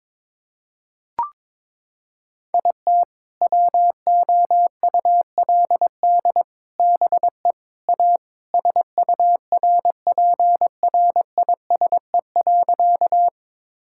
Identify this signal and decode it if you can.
Morse code, heard as a keyed tone, about 21 wpm, 700 Hz. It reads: IT WOULD BE A SURPRISE.
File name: Sentences with 4-Word Phrases - 22wpm